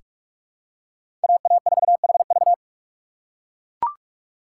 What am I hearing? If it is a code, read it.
Morse code, heard as a keyed tone, about 42 wpm, 700 Hz. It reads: AA4HV
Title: K1USN SST QSO Practice - 45wpm